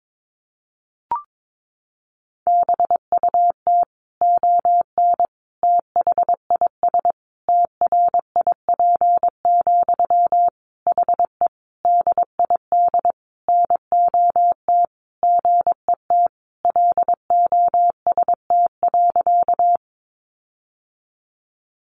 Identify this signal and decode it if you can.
Morse code, heard as a keyed tone, about 22 wpm, 700 Hz. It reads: BUT ON THIS TRIP, HE DID NOT GET LOST.
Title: Sentences from Top 1400 Words 22wpm